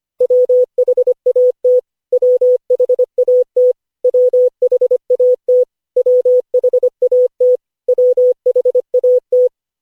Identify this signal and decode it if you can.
Morse code, heard as a keyed tone, about 25 wpm, 500 Hz. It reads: WHAT WHAT WHAT WHAT WHAT
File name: cw / Session5 cw training